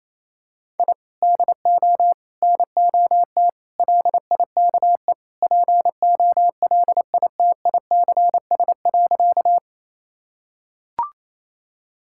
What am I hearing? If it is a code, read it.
Morse code, heard as a keyed tone, about 28 wpm, 700 Hz. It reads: I DO NOT LIKE POLITICS.
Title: Sentences from Top 1000 Words - Review Part 6 - 28wpm